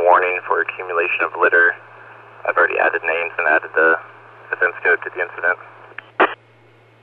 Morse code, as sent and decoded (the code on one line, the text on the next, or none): none